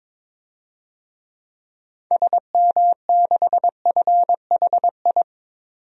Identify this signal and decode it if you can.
Morse code, heard as a keyed tone, about 23 wpm, 700 Hz. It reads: SM6FHI
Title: DXCC-100 Call Signs - 6 Characters - 22wpm